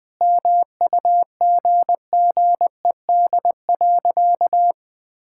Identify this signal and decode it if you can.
Morse code, heard as a keyed tone, about 21 wpm, 700 Hz. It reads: MUGGED.
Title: All-Age Appropriate Jokes - 20wpm